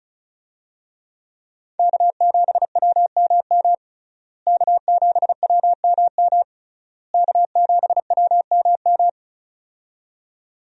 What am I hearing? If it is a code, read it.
Morse code, heard as a keyed tone, about 36 wpm, 700 Hz. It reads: K7WMM K7WMM K7WMM
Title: USA Call Signs - 5 Characters - Repeated 3x - 35wpm